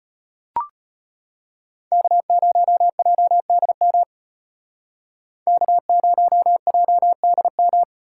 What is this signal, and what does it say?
Morse code, heard as a keyed tone, about 32 wpm, 700 Hz. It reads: K0JDM K0JDM